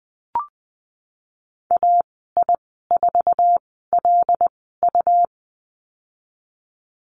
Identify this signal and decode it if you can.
Morse code, heard as a keyed tone, about 20 wpm, spaced as for 15 wpm, 700 Hz. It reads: AI4LU
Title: 122 Lesson 20g - Number 3 - Call Signs 20wpm